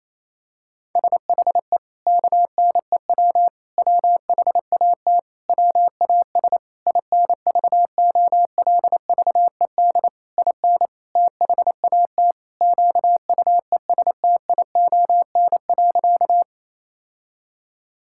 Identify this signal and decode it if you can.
Morse code, heard as a keyed tone, about 28 wpm, 700 Hz. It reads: SHE KNEW WHAT WAS INVOLVED IN THAT QUESTION.
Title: Sentences from Top 1000 Words - Review Part 5 - 28wpm